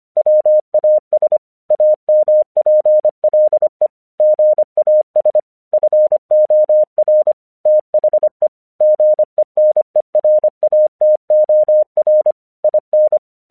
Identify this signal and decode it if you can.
Morse code, heard as a keyed tone, about 25 wpm, 610 Hz. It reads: WAS AMPLE GAS FOR THE GENERATOR IN
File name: 1 English 25wpm / AQC05